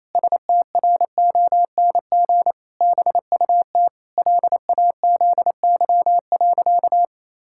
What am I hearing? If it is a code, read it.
Morse code, heard as a keyed tone, about 28 wpm, 700 Hz. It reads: STRONG BUT LAZY.
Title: Pangram Challenge v2 - 28wpm